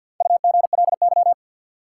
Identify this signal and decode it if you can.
Morse code, heard as a keyed tone, about 48 wpm, 700 Hz. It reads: UGLY
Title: Four Letter Words - 50wpm